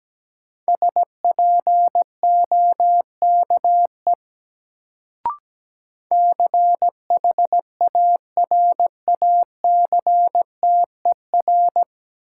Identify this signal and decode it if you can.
Morse code, heard as a keyed tone, about 17 wpm, 700 Hz. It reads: SPOKE CHARACTER
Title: Sentences from Top 1200 Words - New Word Drill Alt - 17wpm